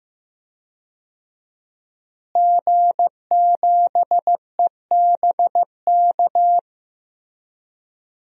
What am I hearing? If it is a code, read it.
Morse code, heard as a keyed tone, about 15 wpm, 700 Hz. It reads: G7EBK
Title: DXCC-100 Call Signs - 5 Characters - Repeated 2x - 15wpm